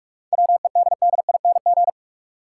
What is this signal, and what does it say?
Morse code, heard as a keyed tone, about 46 wpm, 700 Hz. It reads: WEDDING
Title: Seven Letter Words - 45wpm